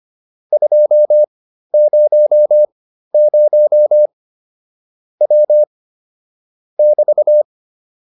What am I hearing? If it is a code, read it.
Morse code, heard as a keyed tone, about 25 wpm, spaced as for 13 wpm, 600 Hz. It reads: 200 W <BT>
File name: QSO 407 13